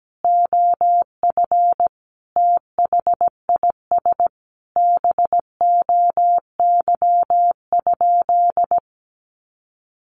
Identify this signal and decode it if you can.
Morse code, heard as a keyed tone, about 17 wpm, 700 Hz. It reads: OF THIS BOY?
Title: Sentences from Top 1500 Words 17wpm